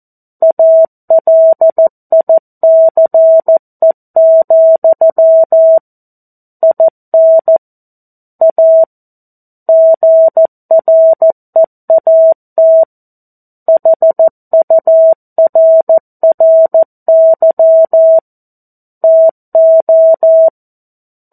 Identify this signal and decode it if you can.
Morse code, heard as a keyed tone, about 14 wpm, 650 Hz. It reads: ALICE, IN A GREAT HURRY TO
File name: 02-Chapter-02 14 WPM